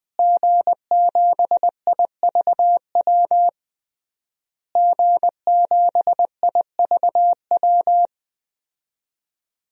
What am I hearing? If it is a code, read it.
Morse code, heard as a keyed tone, about 20 wpm, 700 Hz. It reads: G7IVW G7IVW
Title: DXCC-100 Call Signs - 5 Characters - Repeated 3x - 20wpm